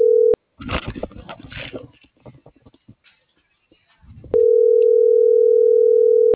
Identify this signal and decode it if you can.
Ringback tone (North America), 440 + 480 Hz.